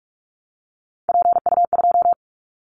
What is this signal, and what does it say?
Morse code, heard as a keyed tone, about 44 wpm, 700 Hz. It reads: PU2
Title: Callsign Prefixes for Top 100 DXCC - 45wpm